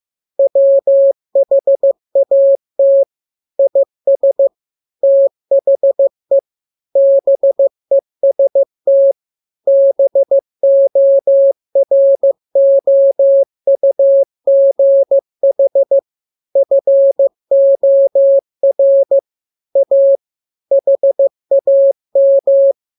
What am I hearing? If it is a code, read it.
Morse code, heard as a keyed tone, about 15 wpm, 550 Hz. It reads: WHAT IS THE BEST BOROUGH FOR A HAM
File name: jokes / joke 15 15 0000